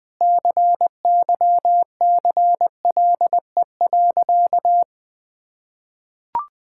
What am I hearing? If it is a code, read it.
Morse code, heard as a keyed tone, about 20 wpm, 700 Hz. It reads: CYCLE.
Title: Sentences from Top 1000 Words - Review Part 6 - 20wpm